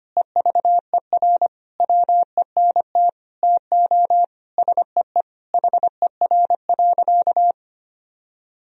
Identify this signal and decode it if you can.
Morse code, heard as a keyed tone, about 25 wpm, 700 Hz. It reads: EVER WENT TO SEE HER.
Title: Sentences from Top 1000 Words - Review Part 1 - 25wpm